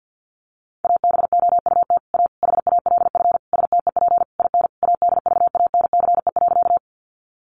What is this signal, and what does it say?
Morse code, heard as a keyed tone, about 50 wpm, 700 Hz. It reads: ABOUT A HALF STEP IN ADVANCE.
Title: Sentences from Top 1400 Words 50wpm